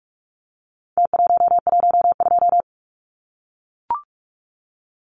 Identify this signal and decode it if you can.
Morse code, heard as a keyed tone, about 44 wpm, 700 Hz. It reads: T112
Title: Numbers - Contest Style - 4-Digit 45wpm